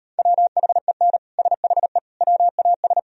Morse code, heard as a keyed tone, about 38 wpm, 700 Hz. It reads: WHEN SHE WAS